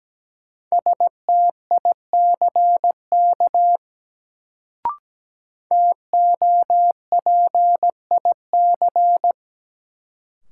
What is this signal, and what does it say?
Morse code, heard as a keyed tone, about 17 wpm, 700 Hz. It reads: STICK TOPIC
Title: Sentences from Top 1200 Words - New Word Drill - 17wpm